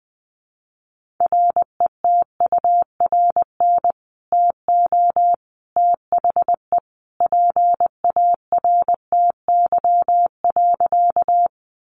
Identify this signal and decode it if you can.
Morse code, heard as a keyed tone, about 20 wpm, 700 Hz. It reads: RETURN TO THE PARTY.
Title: Sentences from Top 1400 Words 20wpm